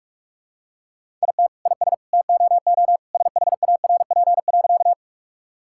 Morse code, heard as a keyed tone, about 46 wpm, 700 Hz. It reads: IT IS TOO SHARP.
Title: Sentences from Top 1600 Words 45wpm